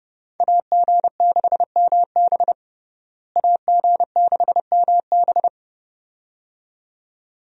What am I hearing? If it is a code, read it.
Morse code, heard as a keyed tone, about 30 wpm, 700 Hz. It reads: AG6MB AG6MB
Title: USA Call Signs - 5 Characters - Repeated 2x - 30wpm